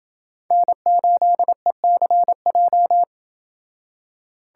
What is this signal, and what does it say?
Morse code, heard as a keyed tone, about 28 wpm, 700 Hz. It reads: N8ECJ